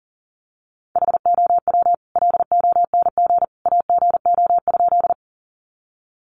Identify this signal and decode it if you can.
Morse code, heard as a keyed tone, about 37 wpm, 700 Hz. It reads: HOW LONG AGO?
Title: Sentences from Top 500 Words 40wpm